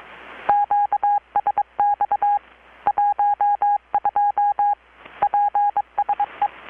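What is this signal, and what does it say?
Morse code, heard as a keyed tone, about 23 wpm, 820 Hz. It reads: QSX 12 PSE
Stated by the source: Morse from KPH's Night of Nights XX maritime radiotelegraphy event, received in Chiapas, Mexico about 3,800 km away on a Kenwood R-600 with a grounded 10 m long-wire antenna.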